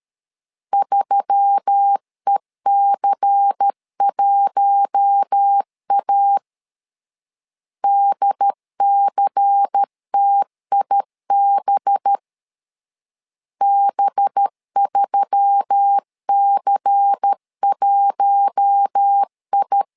Morse code, heard as a keyed tone, about 13 wpm, 800 Hz. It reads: 3EC1A DCTIB B3C1I
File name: LEZIONI / 5 LBVC / 5 LBVC 12wpm 1955